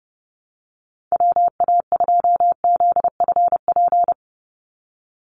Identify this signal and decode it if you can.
Morse code, heard as a keyed tone, about 30 wpm, 700 Hz. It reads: WA2ZFP
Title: USA Call Signs 30wpm